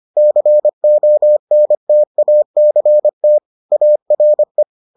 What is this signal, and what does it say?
Morse code, heard as a keyed tone, about 25 wpm, 600 Hz. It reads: CONTACT ARE